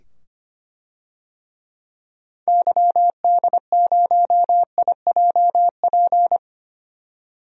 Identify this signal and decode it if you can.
Morse code, heard as a keyed tone, about 25 wpm, 700 Hz. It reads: YD0IJP
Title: DXCC-100 Call Signs - 6 Characters - 25wpm